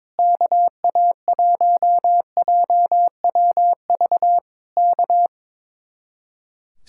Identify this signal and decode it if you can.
Morse code, heard as a keyed tone, about 22 wpm, 700 Hz. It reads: KA1JWV K
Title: Speed-Racing Common QSO Phrases - 22wpm